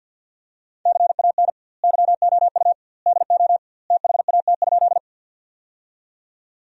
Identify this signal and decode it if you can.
Morse code, heard as a keyed tone, about 50 wpm, 700 Hz. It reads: CAN YOU DO THAT?
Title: 2x Word-Spacing - Sentences from Top 100 Words - 50wpm